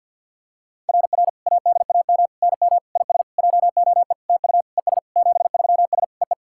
Morse code, heard as a keyed tone, about 50 wpm, 690 Hz. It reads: RR ADAM NM IS JOE TU ES 73S EE
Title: SOTA QSO Practice 50wpm